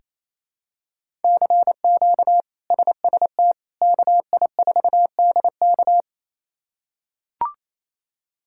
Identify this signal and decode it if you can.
Morse code, heard as a keyed tone, about 28 wpm, 700 Hz. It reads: CQ SST KI4DK
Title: K1USN SST QSO Practice - 28wpm